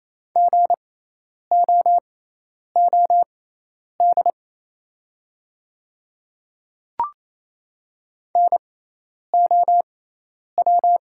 Morse code, heard as a keyed tone, about 27 wpm, spaced as for 10 wpm, 690 Hz. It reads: GOOD NOW
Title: 6x Character Spacing - Top 500 Words - 28wpm